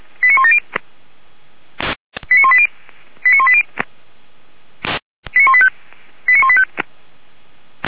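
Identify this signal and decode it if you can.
Five-tone selective call (ZVEI), heard as quick runs of five tones, tones of about 70 ms. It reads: 89170 89180 89180 89185 89185
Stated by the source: Five-tone ZVEI selective call received on SDR, fire brigade feuerwehr-orlamuende, code 89138.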